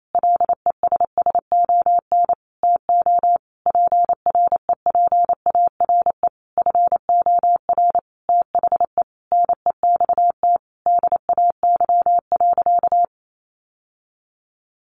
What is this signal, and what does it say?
Morse code, heard as a keyed tone, about 28 wpm, 700 Hz. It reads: LESSON TO PREPARE FOR THE NEXT DAY.
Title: Sentences from Top 1200 Words 28wpm